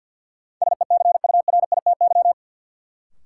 Morse code, heard as a keyed tone, about 53 wpm, 700 Hz. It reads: SECURITY
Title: Sentences from Top 1400 Words - New Word Drill Alt - 50wpm